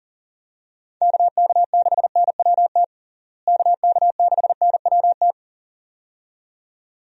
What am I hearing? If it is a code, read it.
Morse code, heard as a keyed tone, about 40 wpm, 700 Hz. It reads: KK6NWT KK6NWT
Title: USA Call Signs - 6 Characters - Repeated 2x - 40wpm